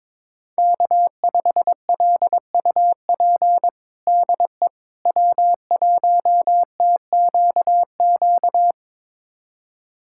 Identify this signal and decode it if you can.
Morse code, heard as a keyed tone, about 22 wpm, 700 Hz. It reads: K5LUP DE W1TQQ
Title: USA Callsign DE Callsign - 5 Characters - 22wpm